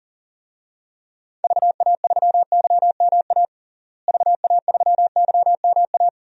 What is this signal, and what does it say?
Morse code, heard as a keyed tone, about 40 wpm, 700 Hz. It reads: VA3YMA VA3YMA
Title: DXCC-100 Call Signs - 6 Characters - Repeated 2x - 40wpm